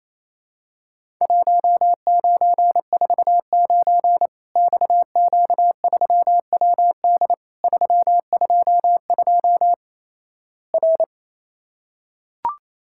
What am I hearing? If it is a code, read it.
Morse code, heard as a keyed tone, about 27 wpm, 700 Hz. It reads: 1949 XQ3WD 322 R
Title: Worked All Europe DX Contest - QTC Exchange - 28wpm